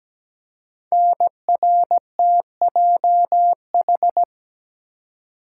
Morse code, heard as a keyed tone, about 17 wpm, 700 Hz. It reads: NRTJH
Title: Code Groups - 5 Letters - 17wpm